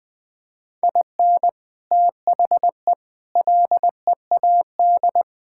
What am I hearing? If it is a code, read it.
Morse code, heard as a keyed tone, about 20 wpm, 700 Hz. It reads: IN THE LEAD
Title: Sets of 6 Words - 20wpm